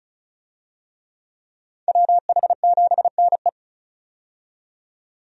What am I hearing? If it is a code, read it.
Morse code, heard as a keyed tone, about 36 wpm, 700 Hz. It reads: WH7NE
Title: USA Call Signs - 5 Characters - 35wpm